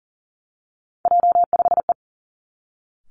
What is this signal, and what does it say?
Morse code, heard as a keyed tone, about 38 wpm, 700 Hz. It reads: J5E
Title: Three Challenging Letter-Number Combinations 40wpm